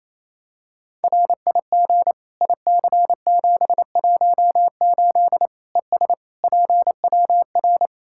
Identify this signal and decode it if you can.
Morse code, heard as a keyed tone, about 28 wpm, 700 Hz. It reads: RIG IC718 ES PWR
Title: Practice QSO - Exchange 2 - Info - 28wpm